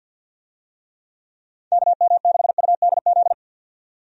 Morse code, heard as a keyed tone, about 51 wpm, 700 Hz. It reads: KM6UDZ
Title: USA Call Signs - 6 Characters - 50wpm